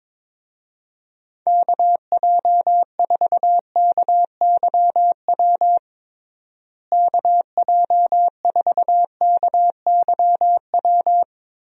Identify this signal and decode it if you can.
Morse code, heard as a keyed tone, about 22 wpm, 700 Hz. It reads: KJ4KYW KJ4KYW
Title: USA Call Signs - 6 Characters - Repeated 3x - 22wpm